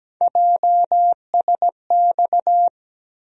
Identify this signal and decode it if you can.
Morse code, heard as a keyed tone, about 17 wpm, 700 Hz. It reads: JSX
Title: USA Call Sign Postfixes - 3 Letters 17wpm